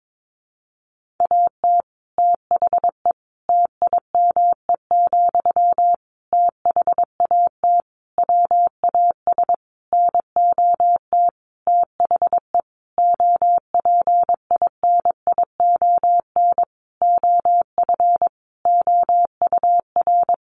Morse code, heard as a keyed tone, about 22 wpm, 700 Hz. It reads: AT THE TIME, THAT WAS NOT THE OPINION OF OUR
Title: Sentences from Top 1400 Words 22wpm